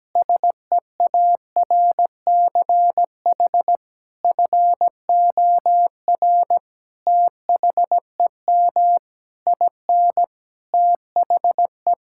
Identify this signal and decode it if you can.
Morse code, heard as a keyed tone, about 17 wpm, 700 Hz. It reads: SEARCH FOR THEM IN THE
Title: Sentences from Top 1000 Words - Review Part 5 - 17wpm